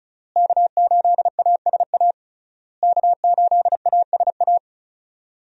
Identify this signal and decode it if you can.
Morse code, heard as a keyed tone, about 34 wpm, 700 Hz. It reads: K8ASA K8ASA
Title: USA Call Signs - 5 Characters - Repeated 3x - 35wpm